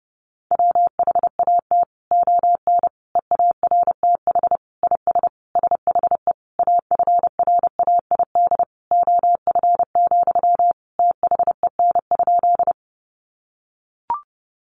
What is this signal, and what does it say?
Morse code, heard as a keyed tone, about 28 wpm, 700 Hz. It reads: WHAT ON EARTH IS SHE AFRAID OF, THEN?
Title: Sentences from Top 1500 Words 30wpm